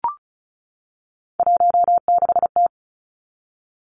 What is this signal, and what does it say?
Morse code, heard as a keyed tone, about 36 wpm, 700 Hz. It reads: 16T
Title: Three Challenging Letter-Number Combinations 35wpm